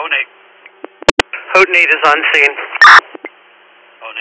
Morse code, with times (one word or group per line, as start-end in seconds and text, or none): none